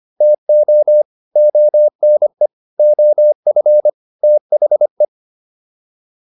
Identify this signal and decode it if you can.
Morse code, heard as a keyed tone, about 25 wpm, 600 Hz. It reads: TO ONE OF THE